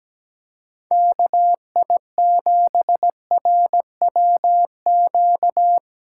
Morse code, heard as a keyed tone, about 17 wpm, 700 Hz. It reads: KI7RWQ
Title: USA Callsign DE Callsign 17wpm